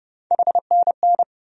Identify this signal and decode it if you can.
Morse code, heard as a keyed tone, about 30 wpm, 700 Hz. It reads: HNN
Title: USA Call Sign Postfixes - 3 Letters 30wpm